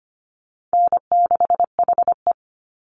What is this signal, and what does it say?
Morse code, heard as a keyed tone, about 25 wpm, 700 Hz. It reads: N6HE
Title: Easy Call Signs Repeated 2x - 25wpm